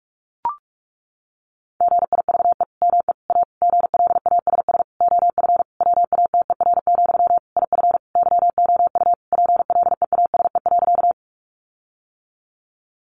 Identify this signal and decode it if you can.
Morse code, heard as a keyed tone, about 44 wpm, 700 Hz. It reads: GIVE ME A GLASS OF WATER, IF YOU PLEASE.
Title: Sentences from Top 1500 Words 45wpm